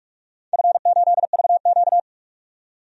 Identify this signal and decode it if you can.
Morse code, heard as a keyed tone, about 44 wpm, 690 Hz. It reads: F8VQ
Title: DXCC-100 Call Signs - 4 Characters - 45wpm